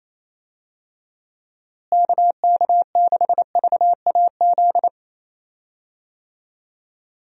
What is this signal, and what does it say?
Morse code, heard as a keyed tone, about 27 wpm, 700 Hz. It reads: KK6VAZ
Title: USA Call Signs - 6 Characters - Repeated 3x - 28wpm